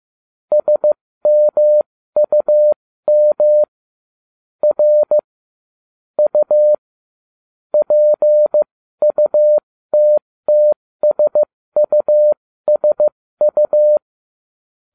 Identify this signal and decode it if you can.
Morse code, heard as a keyed tone, about 15 wpm, spaced as for 13 wpm, 600 Hz. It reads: SMUM R U PUTTSUSU
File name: cw / ubacw07 1